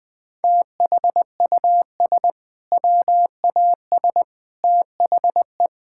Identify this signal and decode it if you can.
Morse code, heard as a keyed tone, about 20 wpm, 700 Hz. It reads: THUS WAS THE
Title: Sentences from Top 1000 Words - Review Part 5 - 20wpm